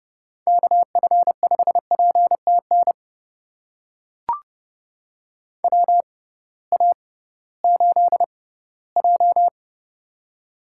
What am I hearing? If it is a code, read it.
Morse code, heard as a keyed tone, about 30 wpm, spaced as for 17 wpm, 700 Hz. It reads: KF5PTN WA8J
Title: USA Call Signs - 6x Character Spacing - 30wpm